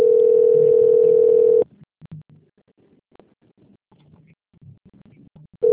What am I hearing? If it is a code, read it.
Ringback tone (North America), 440 + 480 Hz.